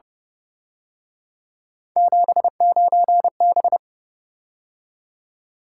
Morse code, heard as a keyed tone, about 29 wpm, 700 Hz. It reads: 79B
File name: Speed-Racing - Three Letter-Number Combinations - 25wpm